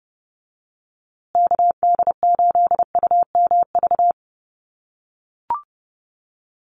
Morse code, heard as a keyed tone, about 30 wpm, 700 Hz. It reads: KD8UMV
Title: USA Call Signs Repeated 2x 30wpm